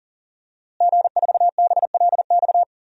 Morse code, heard as a keyed tone, about 37 wpm, 700 Hz. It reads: G4BLX
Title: DXCC-100 Call Signs - 5 Characters - Repeated 2x - 40wpm